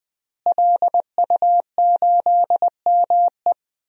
Morse code, heard as a keyed tone, about 20 wpm, 700 Hz. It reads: LU8ME
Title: DXCC-100 Call Signs - 5 Characters - Repeated 2x - 20wpm